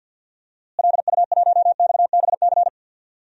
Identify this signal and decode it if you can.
Morse code, heard as a keyed tone, about 48 wpm, 700 Hz. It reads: LU1XBC
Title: DXCC-100 Call Signs - 6 Characters - Repeated 3x - 50wpm